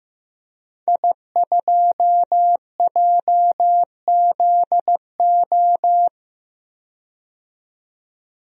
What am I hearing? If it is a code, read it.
Morse code, heard as a keyed tone, about 15 wpm, 700 Hz. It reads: I2JZO
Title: DXCC-100 Call Signs - 5 Characters - 15wpm